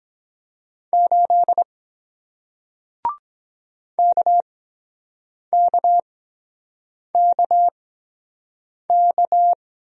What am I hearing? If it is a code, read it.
Morse code, heard as a keyed tone, about 23 wpm, spaced as for 7 wpm, 700 Hz. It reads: 8 KKKK